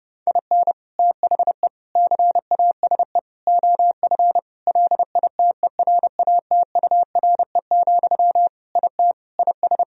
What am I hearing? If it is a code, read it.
Morse code, heard as a keyed tone, about 30 wpm, 700 Hz. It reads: IN THE CASE OF LITERATURE, IT IS
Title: Sentences from Top 1000 Words - Review Part 3 - 30wpm